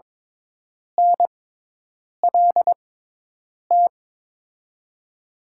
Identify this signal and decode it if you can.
Morse code, heard as a keyed tone, about 21 wpm, spaced as for 8 wpm, 700 Hz. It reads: NLT
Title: USA Call Signs - 6x Character Spacing - 22wpm